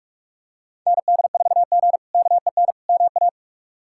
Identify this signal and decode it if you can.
Morse code, heard as a keyed tone, about 44 wpm, 700 Hz. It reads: ND4G KEN MA
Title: K1USN SST QSO Practice - 45wpm